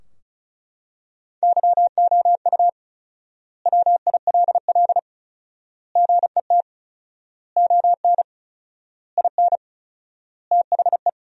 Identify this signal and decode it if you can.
Morse code, heard as a keyed tone, about 36 wpm, 700 Hz. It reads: YOU WILL GET ON IN THE
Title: 4x Word-Spacing - Sentences from Top 200 Words - 35wpm